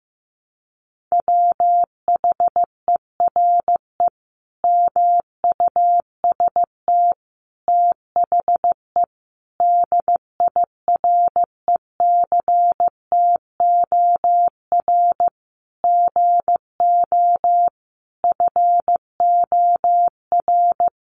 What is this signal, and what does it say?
Morse code, heard as a keyed tone, about 15 wpm, 700 Hz. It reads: WHERE MUST THE DIRECTOR GO FOR
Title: Sentences from Top 1300 Words 15wpm